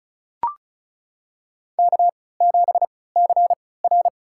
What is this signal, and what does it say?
Morse code, heard as a keyed tone, about 36 wpm, spaced as for 20 wpm, 700 Hz. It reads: K7CR
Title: USA Call Signs - 4 Characters - 3x Character Spacing - 35wpm